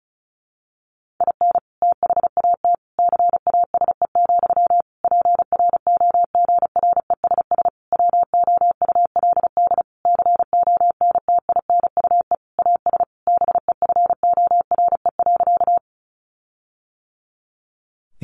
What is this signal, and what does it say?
Morse code, heard as a keyed tone, about 34 wpm, 700 Hz. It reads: IN THAT CASE, PROGRESS WOULD CONTINUE AS BEFORE.
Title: Sentences from Top 1400 Words 35wpm